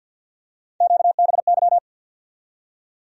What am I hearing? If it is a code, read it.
Morse code, heard as a keyed tone, about 51 wpm, 700 Hz. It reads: QBY
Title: USA Call Sign Postfixes - 3 Letters 50wpm